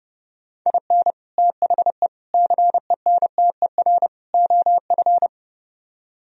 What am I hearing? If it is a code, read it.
Morse code, heard as a keyed tone, about 30 wpm, 700 Hz. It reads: IN THE CENTER OF